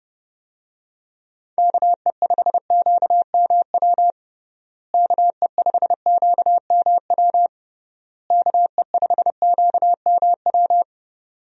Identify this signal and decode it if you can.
Morse code, heard as a keyed tone, about 30 wpm, 700 Hz. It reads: KE5QMW KE5QMW KE5QMW
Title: USA Call Signs - 6 Characters - Repeated 3x - 30wpm